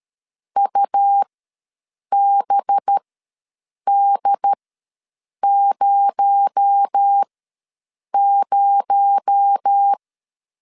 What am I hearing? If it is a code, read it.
Morse code, heard as a keyed tone, about 13 wpm, spaced as for 7 wpm, 800 Hz. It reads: UBD00
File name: LEZIONI / 6 PKX0 / 6 PKX0 ws 12wpm 3448